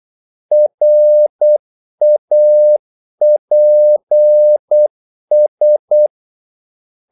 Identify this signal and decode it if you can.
Morse code, heard as a keyed tone, about 8 wpm, 600 Hz. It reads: RAPS
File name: Korte ord DK 8wpm